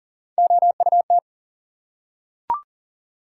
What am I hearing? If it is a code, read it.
Morse code, heard as a keyed tone, about 42 wpm, 700 Hz. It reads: OUT
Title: Three Letter Words - 40wpm